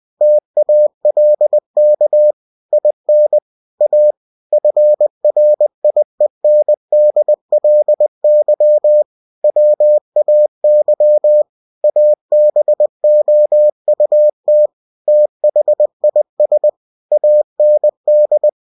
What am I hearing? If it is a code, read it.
Morse code, heard as a keyed tone, about 20 wpm, 600 Hz. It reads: TALK IN A FRIENDLY WAY ABOUT THIS AND
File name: cw-books / winnie-the-pooh / winnie-20wpm-page-0034